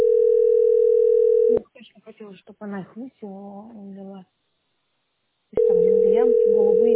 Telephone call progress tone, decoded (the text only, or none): ringback tone